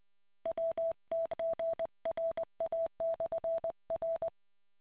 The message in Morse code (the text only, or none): WTGRA/R